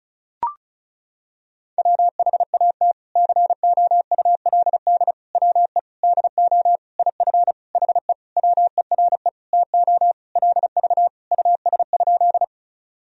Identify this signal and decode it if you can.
Morse code, heard as a keyed tone, about 36 wpm, 700 Hz. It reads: WHAT COULD WE DO IF HE WERE TO LV US?
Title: Sentences from Top 400 Words and Abbreviations 35wpm